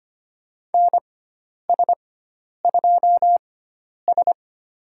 Morse code, heard as a keyed tone, about 25 wpm, spaced as for 10 wpm, 700 Hz. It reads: NS2S
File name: Lesson 16g - Letter c - Call Signs 25wpm